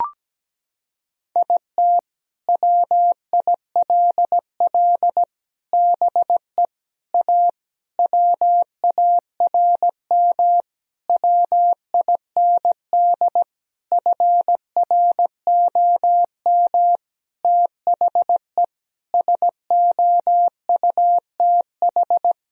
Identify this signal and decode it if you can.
Morse code, heard as a keyed tone, about 17 wpm, 700 Hz. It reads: IT WILL BE A WARM WIND FROM THE SOUTH
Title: Sentences from Top 1000 Words - Review Part 3 - 17wpm